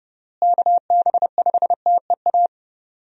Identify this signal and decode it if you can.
Morse code, heard as a keyed tone, about 29 wpm, 700 Hz. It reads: KB5TEA
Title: USA Call Signs - 6 Characters - Repeated 3x - 30wpm